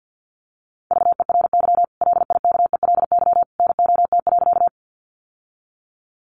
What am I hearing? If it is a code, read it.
Morse code, heard as a keyed tone, about 49 wpm, 700 Hz. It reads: VERY LIKELY NOT.
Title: Sentences from Top 600 Words 50wpm